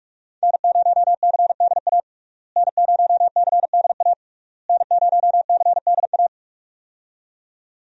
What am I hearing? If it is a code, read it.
Morse code, heard as a keyed tone, about 42 wpm, 700 Hz. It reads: N0CDA N0CDA N0CDA
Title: USA Call Signs - 5 Characters - Repeated 3x - 45wpm